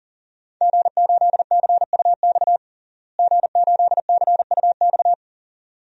Morse code, heard as a keyed tone, about 37 wpm, 700 Hz. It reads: G8CUX G8CUX
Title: DXCC-100 Call Signs - 5 Characters - Repeated 2x - 40wpm